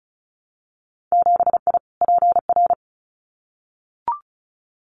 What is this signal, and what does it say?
Morse code, heard as a keyed tone, about 36 wpm, 700 Hz. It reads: 7I PR
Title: Winter Field Day Exchanges Encore 35wpm